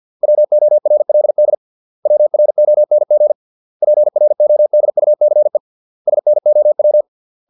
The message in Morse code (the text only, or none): WORLD WRONG PRODUCE SNOW